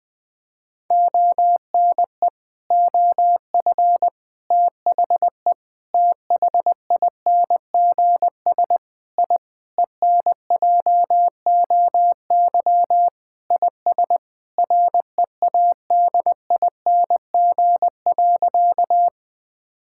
Morse code, heard as a keyed tone, about 20 wpm, 700 Hz. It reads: ONE OF THE THINGS I ENJOY IS READING.
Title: Sentences with 5-Word Phrases - 20wpm